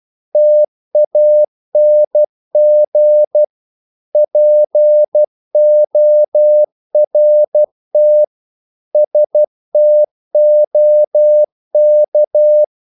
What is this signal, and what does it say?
Morse code, heard as a keyed tone, about 12 wpm, 600 Hz. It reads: TANG PORT STOK
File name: Korte ord DK 12wpm